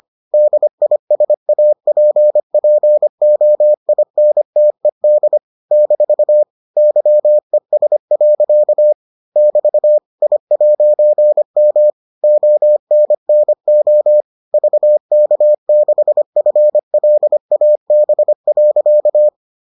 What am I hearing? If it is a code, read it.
Morse code, heard as a keyed tone, about 25 wpm, 600 Hz. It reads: DISAPPOINTED - YES. <BT> I'M ONNO VK6FLAB.